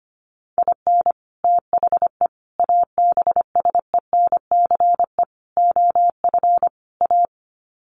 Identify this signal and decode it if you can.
Morse code, heard as a keyed tone, about 25 wpm, 700 Hz. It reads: IN THE ABSENCE OF A
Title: Sets of 5 Words Encore 25wpm